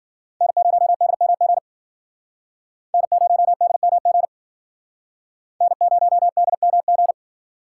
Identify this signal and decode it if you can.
Morse code, heard as a keyed tone, about 54 wpm, 690 Hz. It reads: N0DMG N0DMG N0DMG